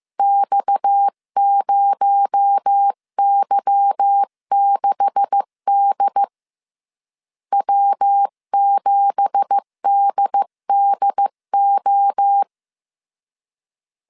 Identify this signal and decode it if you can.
Morse code, heard as a keyed tone, about 15 wpm, 800 Hz. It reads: X0Y6D W7DDO
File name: CW / DET / 1 9 1500C S B 14wpm 3302